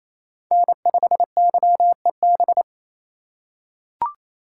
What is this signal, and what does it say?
Morse code, heard as a keyed tone, about 27 wpm, 700 Hz. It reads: N5YEB
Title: USA Call Signs - 5 Characters - 2x Character Spacing - 28wpm